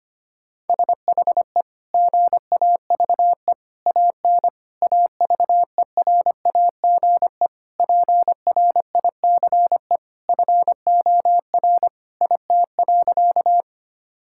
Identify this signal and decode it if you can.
Morse code, heard as a keyed tone, about 25 wpm, 700 Hz. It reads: SHE GAVE AN AVERAGE PRICE FOR IT.
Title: Sentences from Top 1000 Words - Review Part 6 - 25wpm